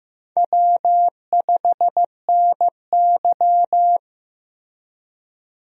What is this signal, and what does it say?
Morse code, heard as a keyed tone, about 15 wpm, 700 Hz. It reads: W5NY
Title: USA Call Signs - 4 Characters - Repeated 3x - 15wpm